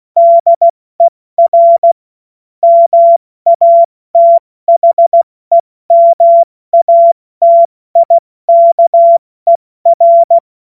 Morse code, heard as a keyed tone, about 16 wpm, spaced as for 14 wpm, 680 Hz. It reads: DER MATHEMATIKER
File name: Morsekurs / Witze / Witz 26 Dezember 2023 70 F80